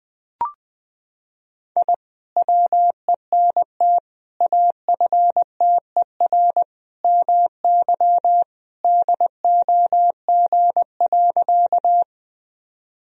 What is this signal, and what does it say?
Morse code, heard as a keyed tone, about 20 wpm, 700 Hz. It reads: I WENT AFTER MY DOG.
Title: Sentences from Top 1000 Words - Review Part 2 - 20wpm